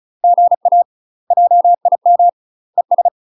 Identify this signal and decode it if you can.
Morse code, heard as a keyed tone, about 35 wpm, 700 Hz. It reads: GA JIM ES
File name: UK MASTER QSOs TRAINING / 35WPM / UK MASTER QSOs 0005